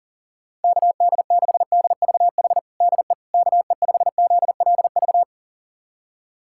Morse code, heard as a keyed tone, about 40 wpm, 700 Hz. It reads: KD6DVH DE KE5ZLV
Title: USA Callsign DE Callsign 40wpm